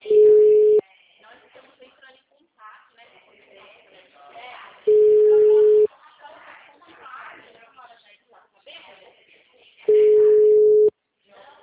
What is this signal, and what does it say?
Ringback tone (Europe, 425 Hz), with speech.